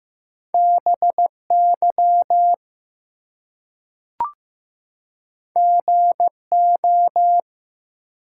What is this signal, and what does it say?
Morse code, heard as a keyed tone, about 15 wpm, 700 Hz. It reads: BY GO
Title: Two Letter Words - 15wpm